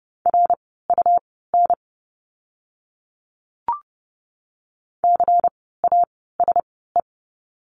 Morse code, heard as a keyed tone, about 30 wpm, spaced as for 17 wpm, 700 Hz. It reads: RUN CASE
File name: Lesson 17f - Letter u - Words 30wpm